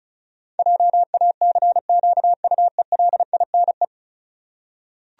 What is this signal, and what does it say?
Morse code, heard as a keyed tone, about 36 wpm, 700 Hz. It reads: JACQUELINE
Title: USA Common Names - 35wpm